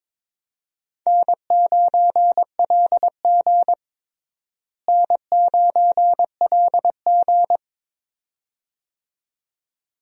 Morse code, heard as a keyed tone, about 22 wpm, 700 Hz. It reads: N9LG N9LG
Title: USA Call Signs - 4 Characters - Repeated 2x - 22wpm